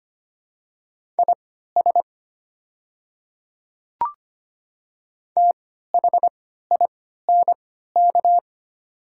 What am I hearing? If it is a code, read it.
Morse code, heard as a keyed tone, about 25 wpm, spaced as for 14 wpm, 700 Hz. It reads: IS THINK
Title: 3x Character Spacing - Top 200 Words - 25wpm